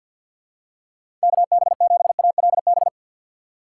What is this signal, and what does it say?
Morse code, heard as a keyed tone, about 50 wpm, 700 Hz. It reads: KB7ALB
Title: USA Call Signs - 6x Character Spacing - 50wpm